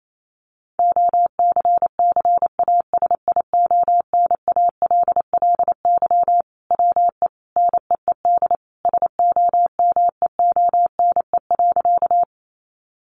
Morse code, heard as a keyed tone, about 28 wpm, 700 Hz. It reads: OCCASIONALLY WE NEED SOMEONE.
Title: Sentences from Top 1300 Words 28wpm